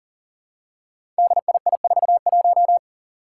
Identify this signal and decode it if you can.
Morse code, heard as a keyed tone, about 40 wpm, 700 Hz. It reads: DII41
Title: Five Challenging Letter-Number Combinations 40wpm